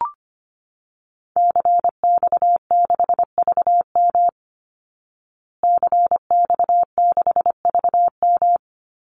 Morse code, heard as a keyed tone, about 25 wpm, 700 Hz. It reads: CX6VM CX6VM
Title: Call Signs Repeated 2x - 25wpm